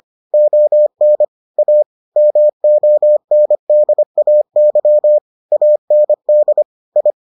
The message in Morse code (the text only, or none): ON A MONDAY AND I